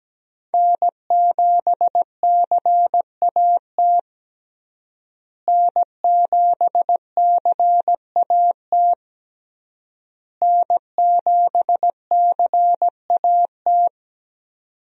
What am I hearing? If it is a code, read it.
Morse code, heard as a keyed tone, about 17 wpm, 700 Hz. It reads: N7CAT N7CAT N7CAT
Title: USA Call Signs - 5 Characters - Repeated 3x - 17wpm